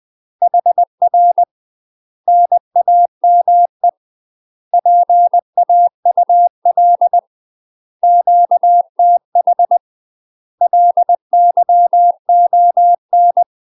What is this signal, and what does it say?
Morse code, heard as a keyed tone, about 20 wpm, 700 Hz. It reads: HR NAME PAUL QTH LYON
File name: UK MASTER QSOs TRAINING / 20WPM / UK MASTER QSOs 0002